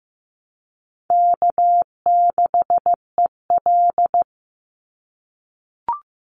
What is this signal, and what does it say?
Morse code, heard as a keyed tone, about 15 wpm, 700 Hz. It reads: K6EL
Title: NA SOTA Chaser Callsigns 15wpm